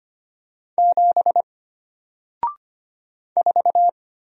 Morse code, heard as a keyed tone, about 25 wpm, 700 Hz. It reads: 7 4
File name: Single-Letter-Number-25wpm